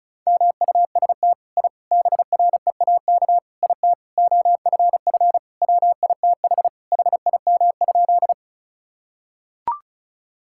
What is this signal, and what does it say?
Morse code, heard as a keyed tone, about 34 wpm, 700 Hz. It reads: MUST I BREAK IT OFF WITH HIM?
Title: Sentences from Top 1000 Words - Review Part 6 - 35wpm